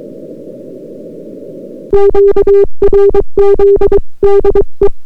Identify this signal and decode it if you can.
Morse code, heard as a keyed tone, about 23 wpm, 400 Hz. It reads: QRZ DE